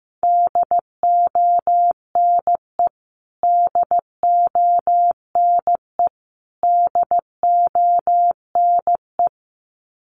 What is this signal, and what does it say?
Morse code, heard as a keyed tone, about 14 wpm, 700 Hz. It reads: DONE DONE DONE
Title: Top 500 Words Repeated 15wpm